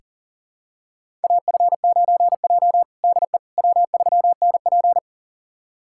Morse code, heard as a keyed tone, about 40 wpm, 700 Hz. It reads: AF9J DE W3NP
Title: USA Callsign DE Callsign - 4 Characters - 40wpm